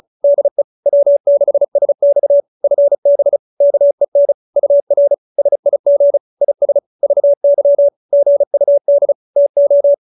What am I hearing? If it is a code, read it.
Morse code, heard as a keyed tone, about 34 wpm, 570 Hz. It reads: DE W6SX FB KEN UR SIG IS VY GUD TO